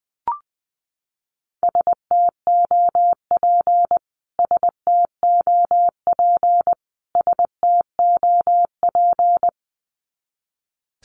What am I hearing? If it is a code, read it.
Morse code, heard as a keyed tone, about 20 wpm, 700 Hz. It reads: STOP STOP STOP
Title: Top 300 Words Repeated 20wpm